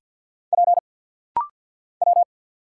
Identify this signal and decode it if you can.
Morse code, heard as a keyed tone, about 50 wpm, 690 Hz. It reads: P W
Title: Single Letter-Number Alt - Rapid-Fire 50wpm